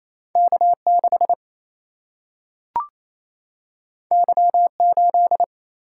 Morse code, heard as a keyed tone, about 28 wpm, 700 Hz. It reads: K 6 Y 8
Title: Single-Letter Single-Number 28wpm